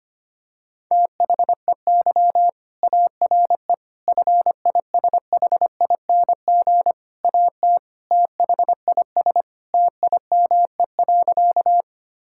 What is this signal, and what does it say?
Morse code, heard as a keyed tone, about 25 wpm, 700 Hz. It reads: THEY ARE FISHING AT THIS TIME.
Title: Sentences from Top 1000 Words - Review Part 5 - 25wpm